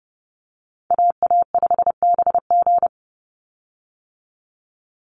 Morse code, heard as a keyed tone, about 30 wpm, 700 Hz. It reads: AA5BG
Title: Call Signs - 30wpm